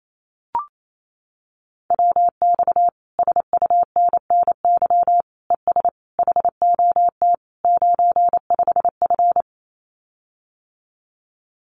Morse code, heard as a keyed tone, about 28 wpm, 700 Hz. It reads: WX SUNNY ES HOT 95F
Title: Common QSO Phrases Encore 28wpm